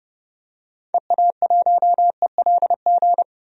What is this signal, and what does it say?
Morse code, heard as a keyed tone, about 32 wpm, 700 Hz. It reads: EA1ELG
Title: DXCC-100 Call Signs - 6 Characters - 30wpm